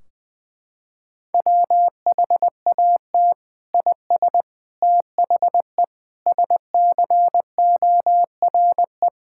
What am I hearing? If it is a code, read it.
Morse code, heard as a keyed tone, about 20 wpm, 700 Hz. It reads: WHAT IS THE SCORE
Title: Sentences from Top 1600 Words 20wpm